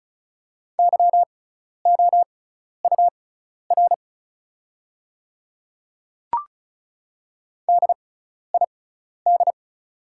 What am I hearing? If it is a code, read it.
Morse code, heard as a keyed tone, about 36 wpm, spaced as for 12 wpm, 700 Hz. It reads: YOUR DID
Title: 6x Character Spacing - Top 100 Words - 35wpm